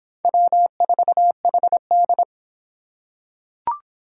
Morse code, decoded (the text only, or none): W4HD